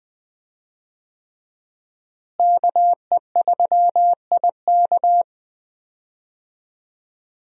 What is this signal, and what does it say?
Morse code, heard as a keyed tone, about 20 wpm, 700 Hz. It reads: KE3IK
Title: USA Call Signs - 5 Characters - Repeated 3x - 20wpm